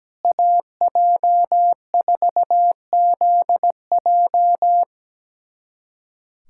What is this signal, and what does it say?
Morse code, heard as a keyed tone, about 17 wpm, 700 Hz. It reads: AJ4ZJ
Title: USA Callsign DE Callsign - 5 Characters - 17wpm